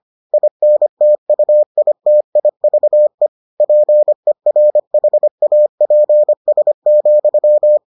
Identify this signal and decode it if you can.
Morse code, heard as a keyed tone, about 25 wpm, 600 Hz. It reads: INTUITIVE PERHAPS,